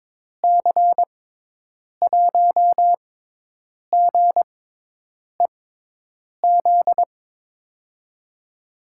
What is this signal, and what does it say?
Morse code, heard as a keyed tone, about 22 wpm, spaced as for 8 wpm, 700 Hz. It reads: C1GEZ
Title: USA Call Signs - 6x Character Spacing - 22wpm